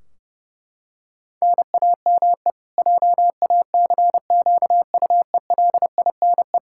Morse code, heard as a keyed tone, about 32 wpm, 700 Hz. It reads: NAME JACQUELINE
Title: Practice QSO - Exchange 1 - Beginning - 30wpm